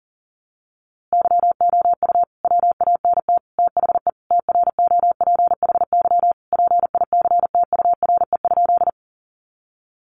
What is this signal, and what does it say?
Morse code, heard as a keyed tone, about 40 wpm, 700 Hz. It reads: YOU WANT THE TROPHY PICTURE?
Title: A Sound Of Thunder 40wpm